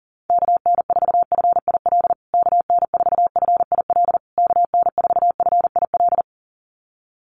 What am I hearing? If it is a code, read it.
Morse code, heard as a keyed tone, about 40 wpm, 700 Hz. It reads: KN4FIL KN4FIL KN4FIL
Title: USA Call Signs Repeated 3x 40wpm